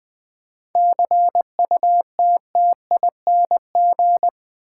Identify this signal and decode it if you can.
Morse code, heard as a keyed tone, about 20 wpm, 700 Hz. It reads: CUTTING
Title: 7 Letter Words - First 3 Letters Given - 20wpm